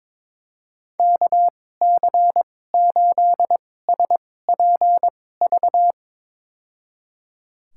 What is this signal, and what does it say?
Morse code, heard as a keyed tone, about 22 wpm, spaced as for 16 wpm, 700 Hz. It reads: KC8SPV
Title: USA Call Signs - 6 Characters - 2x Character Spacing - 22wpm